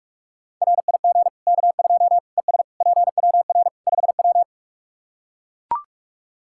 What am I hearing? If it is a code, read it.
Morse code, heard as a keyed tone, about 41 wpm, 700 Hz. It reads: RIG K2 ES PWR 5W
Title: Practice QSO - Exchange 2 - Info - 45wpm